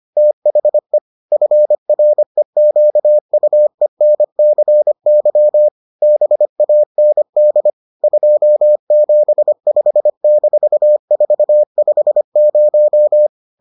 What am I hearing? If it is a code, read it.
Morse code, heard as a keyed tone, about 25 wpm, 600 Hz. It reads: THE FREQUENCY BAND 275-450